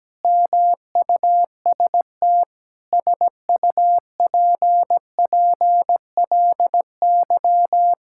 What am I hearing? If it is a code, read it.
Morse code, heard as a keyed tone, about 17 wpm, 700 Hz. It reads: MUST SUPPLY
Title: Sentences from Top 1000 Words - Review Part 5 - 17wpm